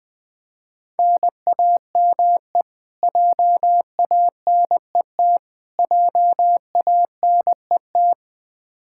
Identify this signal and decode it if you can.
Morse code, heard as a keyed tone, about 20 wpm, 700 Hz. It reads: NAME JANET JANET
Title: Practice QSO - Exchange 1 - Beginning - 20wpm